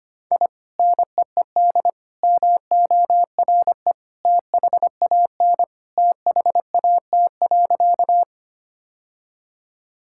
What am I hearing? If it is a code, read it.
Morse code, heard as a keyed tone, about 25 wpm, 700 Hz. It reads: I NEED MORE THAN THAT.
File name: practice / 25wpm / Sentences with 2-Word Phrases - 25wpm